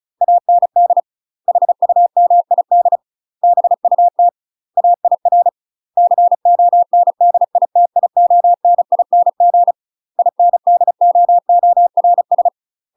Morse code, heard as a keyed tone, about 36 wpm, 700 Hz. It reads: AND HUMID BUT AIR CONDITIONING INDOORS